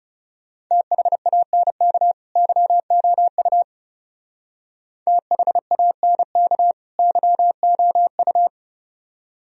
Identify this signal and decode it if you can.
Morse code, heard as a keyed tone, about 36 wpm, 700 Hz. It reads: THANK YOU THANK YOU
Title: Speed-Racing - Sets of 2 Words 30wpm